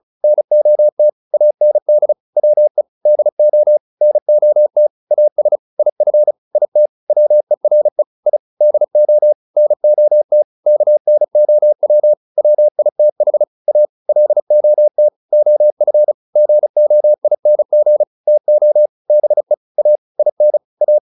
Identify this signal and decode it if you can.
Morse code, heard as a keyed tone, about 35 wpm, 600 Hz. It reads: NOT AND WE DO NOT AS IF IT WERE I DO NOT KNOW WITH A LOT OF GOING TO BE A IN A
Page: advanced-practice-files